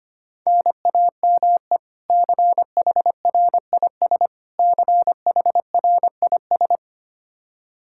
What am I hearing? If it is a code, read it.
Morse code, heard as a keyed tone, about 25 wpm, 700 Hz. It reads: NAME CHRIS CHRIS
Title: Practice QSO - Exchange 1 - Beginning - 25wpm